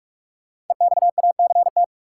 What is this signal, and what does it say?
Morse code, heard as a keyed tone, about 46 wpm, 700 Hz. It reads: EXACT
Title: Sentences from Top 1300 Words - New Word Drill - 45wpm